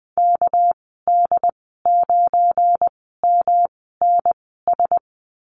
Morse code, heard as a keyed tone, about 20 wpm, spaced as for 14 wpm, 700 Hz. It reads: KD9MNS
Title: 245 Lesson 39g - Letter z - Call Signs 20wpm